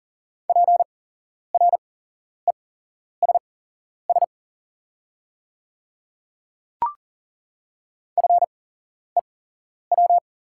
Morse code, heard as a keyed tone, about 39 wpm, spaced as for 11 wpm, 690 Hz. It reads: PRESS FEW
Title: 8x Character Spacing - Top 300 Words - 40wpm